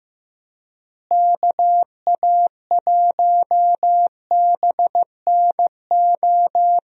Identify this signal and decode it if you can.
Morse code, heard as a keyed tone, about 15 wpm, 700 Hz. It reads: KA1BNO
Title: USA Call Signs - 6 Characters - Repeated 3x - 15wpm